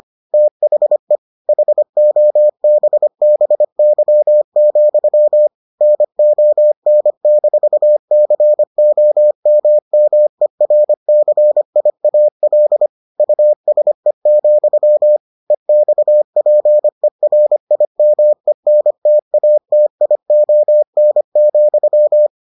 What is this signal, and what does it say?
Morse code, heard as a keyed tone, about 25 wpm, 600 Hz. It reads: THE HOBBY, NON-COMMERCIAL USE, EXPERIMENTATION,